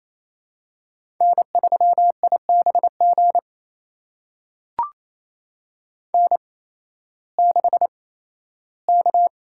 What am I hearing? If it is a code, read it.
Morse code, heard as a keyed tone, about 28 wpm, spaced as for 15 wpm, 700 Hz. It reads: N3IBG N6K
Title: USA Call Signs - 5 Characters - 8x Character Spacing - 28wpm